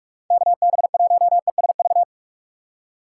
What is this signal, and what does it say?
Morse code, heard as a keyed tone, about 56 wpm, 700 Hz. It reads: KB1ESV
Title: Speed Racing - USA 2x3 Call Signs 45wpm